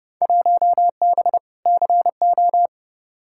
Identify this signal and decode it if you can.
Morse code, heard as a keyed tone, about 32 wpm, 700 Hz. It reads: 1B CO
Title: Field Day Exchanges - 30wpm